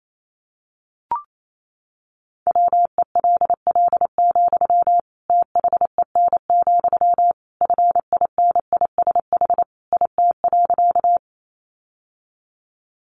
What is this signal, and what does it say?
Morse code, heard as a keyed tone, about 28 wpm, 700 Hz. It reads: WELL, THEN, FINISH IT.
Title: Sentences from Top 1200 Words 28wpm